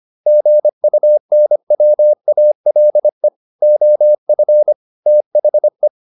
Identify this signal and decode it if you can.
Morse code, heard as a keyed tone, about 25 wpm, 600 Hz. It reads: GUNWALE OF THE